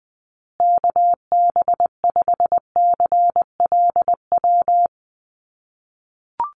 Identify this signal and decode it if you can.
Morse code, heard as a keyed tone, about 20 wpm, 700 Hz. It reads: KB5CLW
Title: USA Call Signs Repeated 2x 20wpm